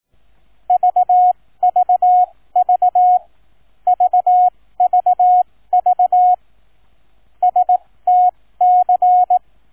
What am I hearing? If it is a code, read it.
Morse code, heard as a keyed tone, about 18 wpm, spaced as for 15 wpm, 730 Hz. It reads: VVV VVV STC
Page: Shortened example of QNC Sound file